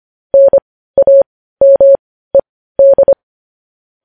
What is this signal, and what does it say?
Morse code, heard as a keyed tone, about 25 wpm, spaced as for 15 wpm, 550 Hz. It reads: NAMED
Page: news headlines in morse code at 15 wpm